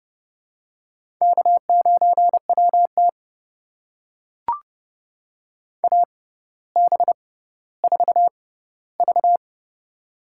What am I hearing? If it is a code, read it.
Morse code, heard as a keyed tone, about 30 wpm, spaced as for 15 wpm, 700 Hz. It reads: K9WT AB4V
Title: USA Call Signs - 4 Characters - 6x Character Spacing - 30wpm